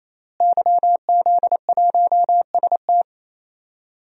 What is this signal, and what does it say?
Morse code, heard as a keyed tone, about 27 wpm, 700 Hz. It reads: YZ1ST
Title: DXCC-100 Call Signs - 5 Characters - Repeated 3x - 28wpm